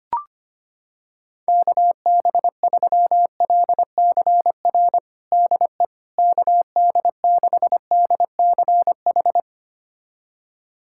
Morse code, heard as a keyed tone, about 25 wpm, 700 Hz. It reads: KB3LCR DE KD6DCH
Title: USA Callsign DE Callsign - 6 Characters - 25wpm